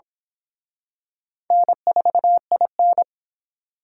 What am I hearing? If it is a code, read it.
Morse code, heard as a keyed tone, about 27 wpm, 700 Hz. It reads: N4IN
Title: Speed Racing - USA 1x2 Call Signs 22wpm